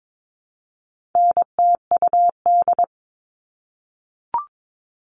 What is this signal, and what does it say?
Morse code, heard as a keyed tone, about 22 wpm, 700 Hz. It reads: NTUD
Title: All Cut Numbers - 4-Digit 22wpm